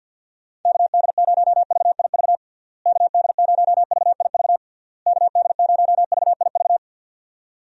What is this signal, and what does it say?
Morse code, heard as a keyed tone, about 50 wpm, 700 Hz. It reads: KD0VIV KD0VIV KD0VIV
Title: USA Call Signs - 6 Characters - Repeated 3x - 50wpm